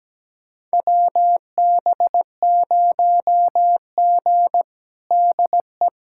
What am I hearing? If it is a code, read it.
Morse code, heard as a keyed tone, about 17 wpm, 700 Hz. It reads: WB0G DE
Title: USA Callsign DE Callsign - 4 Characters - 17wpm